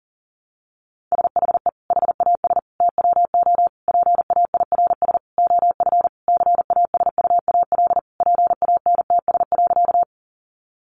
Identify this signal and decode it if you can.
Morse code, heard as a keyed tone, about 40 wpm, 700 Hz. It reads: SHE HAS TWO PAIRS OF CASUAL PANTS.
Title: Additional Easy Phrases 40wpm